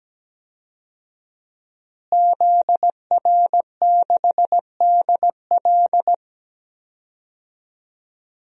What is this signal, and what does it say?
Morse code, heard as a keyed tone, about 17 wpm, 700 Hz. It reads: ZR6DL
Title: DXCC-100 Call Signs - 5 Characters - Repeated 3x - 17wpm